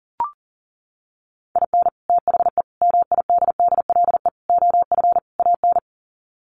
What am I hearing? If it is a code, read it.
Morse code, heard as a keyed tone, about 40 wpm, 700 Hz. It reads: IN THE MIDDLE OF AN
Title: Sets of 5 Words Encore2 40wpm